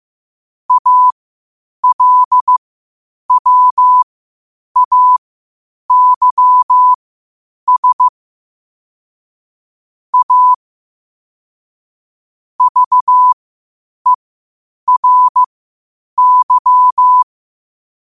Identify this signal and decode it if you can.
Morse code, heard as a keyed tone, about 15 wpm, spaced as for 8 wpm, 1000 Hz. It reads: ALWAYS A VERY
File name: qotdfiles / 8wpm